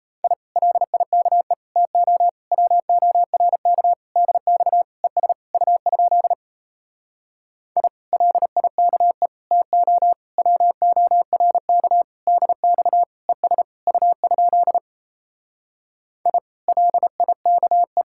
Morse code, heard as a keyed tone, about 38 wpm, 700 Hz. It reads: I LIKE TO WORK DX ES U? I LIKE TO WORK DX ES U? I LIKE